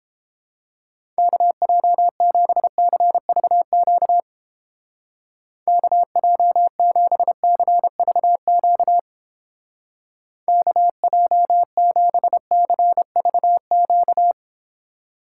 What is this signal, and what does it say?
Morse code, heard as a keyed tone, about 29 wpm, 700 Hz. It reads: KJ7CVQ KJ7CVQ KJ7CVQ